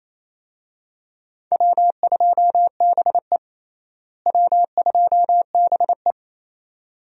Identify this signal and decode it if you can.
Morse code, heard as a keyed tone, about 28 wpm, 700 Hz. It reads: W2BE W2BE
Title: USA Call Signs - 4 Characters - Repeated 2x - 28wpm